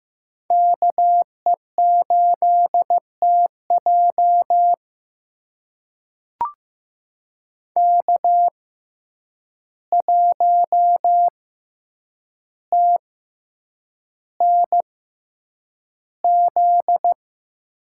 Morse code, heard as a keyed tone, about 15 wpm, spaced as for 8 wpm, 700 Hz. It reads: KE8TJ K1TNZ
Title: USA Call Signs - 5 Characters - 6x Character Spacing - 15wpm